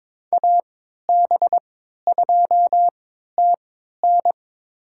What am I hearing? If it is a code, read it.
Morse code, heard as a keyed tone, about 22 wpm, spaced as for 13 wpm, 700 Hz. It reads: AB2TN
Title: USA Call Signs - 5 Characters - 3x Character Spacing - 22wpm